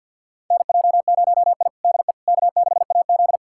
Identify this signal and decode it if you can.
Morse code, heard as a keyed tone, about 49 wpm, 700 Hz. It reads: NJ0I DE K6AZ
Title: USA Callsign DE Callsign - 4 Characters - 50wpm